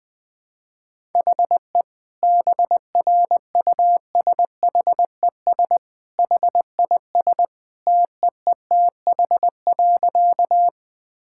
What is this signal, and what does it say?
Morse code, heard as a keyed tone, about 20 wpm, 700 Hz. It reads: HE BRUSHES HIS TEETH.
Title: Easy Sentences with Hints - 20wpm